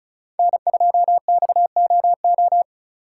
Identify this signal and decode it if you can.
Morse code, heard as a keyed tone, about 36 wpm, 700 Hz. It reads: N2XOO
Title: USA Call Signs - 5 Characters - 2x Character Spacing - 35wpm